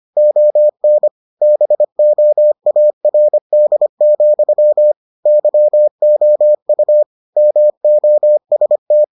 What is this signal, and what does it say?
Morse code, heard as a keyed tone, about 25 wpm, 600 Hz. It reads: ON BOARD, YOU MOST